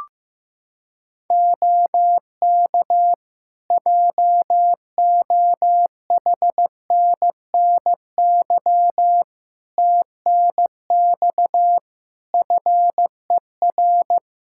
Morse code, heard as a keyed tone, about 15 wpm, 700 Hz. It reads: OK JOHNNY TNX FER
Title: Practice QSO - Exchange 4 - Ending - 15wpm